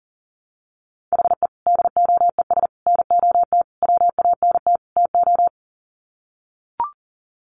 Morse code, heard as a keyed tone, about 40 wpm, 700 Hz. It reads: HE DOES NOT WANT TO
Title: Sets of 5 Words 40wpm